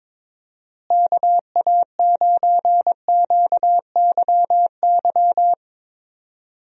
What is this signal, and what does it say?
Morse code, heard as a keyed tone, about 22 wpm, 700 Hz. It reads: KA9QYY
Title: USA Call Signs - 6 Characters - Repeated 2x - 22wpm